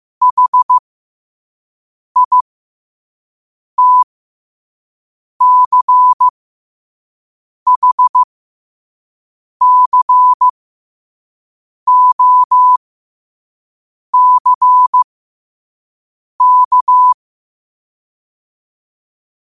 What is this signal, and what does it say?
Morse code, heard as a keyed tone, about 15 wpm, spaced as for 5 wpm, 1000 Hz. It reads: HITCHCOCK
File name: qotdfiles / 5wpm